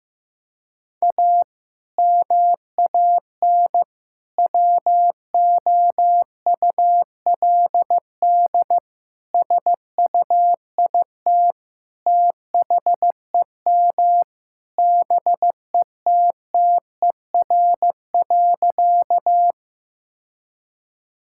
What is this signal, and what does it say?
Morse code, heard as a keyed tone, about 15 wpm, 700 Hz. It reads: A MAN WOULD SUIT THEM BETTER.
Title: Sentences from Top 1600 Words 15wpm